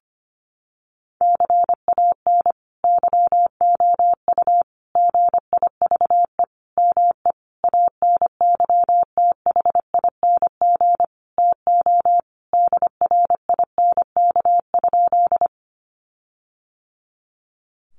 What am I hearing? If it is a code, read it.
Morse code, heard as a keyed tone, about 25 wpm, 700 Hz. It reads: CAN YOU GIVE ME ANYTHING TO DRINK?
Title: Sentences from Top 1500 Words 25wpm